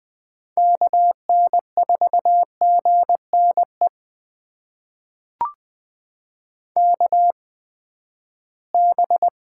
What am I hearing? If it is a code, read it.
Morse code, heard as a keyed tone, about 20 wpm, spaced as for 14 wpm, 700 Hz. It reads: KN4GNE KB
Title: USA Call Signs - 6 Characters - 8x Character Spacing - 20wpm